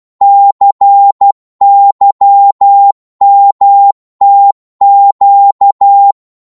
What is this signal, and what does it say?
Morse code, heard as a keyed tone, about 12 wpm, 800 Hz. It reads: CYMTQ
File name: Morse / ltr-12wpm